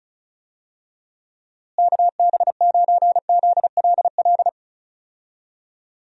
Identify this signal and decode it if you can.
Morse code, heard as a keyed tone, about 36 wpm, 700 Hz. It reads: KB9ZLL
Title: USA Call Signs - 6 Characters - Repeated 3x - 35wpm